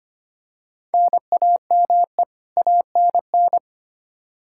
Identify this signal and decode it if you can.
Morse code, heard as a keyed tone, about 25 wpm, 700 Hz. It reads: NAME ANN
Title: K1USN SST QSO Practice 25wpm